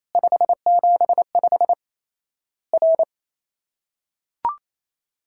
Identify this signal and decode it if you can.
Morse code, heard as a keyed tone, about 28 wpm, 700 Hz. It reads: 575 R
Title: Worked All Europe DX Contest - QTC Exchange - 28wpm